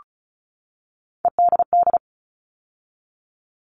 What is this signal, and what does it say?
Morse code, heard as a keyed tone, about 34 wpm, 700 Hz. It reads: EDD
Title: All Cut Numbers - 3-Digit 35wpm